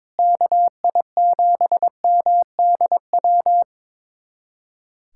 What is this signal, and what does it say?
Morse code, heard as a keyed tone, about 22 wpm, 700 Hz. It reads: KI7MDW
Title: USA Callsign DE Callsign - 6 Characters - 22wpm